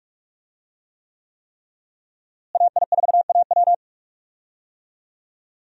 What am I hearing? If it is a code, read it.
Morse code, heard as a keyed tone, about 43 wpm, 700 Hz. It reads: AI4AW
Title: USA Call Signs - 5 Characters - Repeated 3x - 45wpm